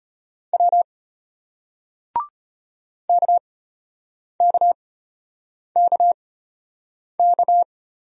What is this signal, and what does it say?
Morse code, heard as a keyed tone, about 33 wpm, spaced as for 8 wpm, 700 Hz. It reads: W KKKK